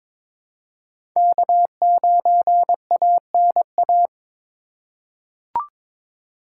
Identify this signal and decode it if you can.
Morse code, heard as a keyed tone, about 22 wpm, 700 Hz. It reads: K9ANA
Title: USA Call Signs - 5 Characters - 4x Character Spacing - 22wpm